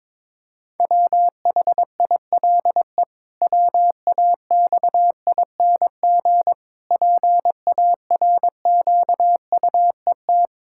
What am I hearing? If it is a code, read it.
Morse code, heard as a keyed tone, about 22 wpm, 700 Hz. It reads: WHILE WAXING PARQUET
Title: Pangram Challenge v2 - 22wpm